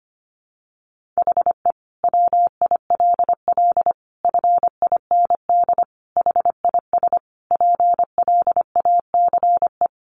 Morse code, heard as a keyed tone, about 25 wpm, 700 Hz. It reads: HE WILL FIND HIS PLACE
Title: Sentences from Top 1400 Words 25wpm